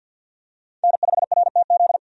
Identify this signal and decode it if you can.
Morse code, heard as a keyed tone, about 52 wpm, 700 Hz. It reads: N5RTZ
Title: USA Call Signs - 5 Characters - 50wpm